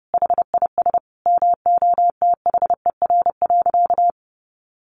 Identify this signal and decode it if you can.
Morse code, heard as a keyed tone, about 30 wpm, 700 Hz. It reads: HIS MOTHER.
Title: Sentences from Top 700 Words 30wpm